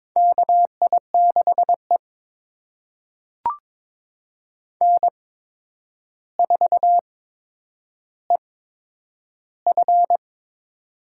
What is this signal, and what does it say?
Morse code, heard as a keyed tone, about 22 wpm, spaced as for 9 wpm, 700 Hz. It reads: KI6E N4EF
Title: USA Call Signs - 4 Characters - 8x Character Spacing - 22wpm